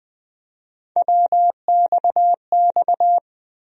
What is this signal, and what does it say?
Morse code, heard as a keyed tone, about 20 wpm, 700 Hz. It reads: WXX
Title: USA Call Sign Postfixes - 3 Letters 20wpm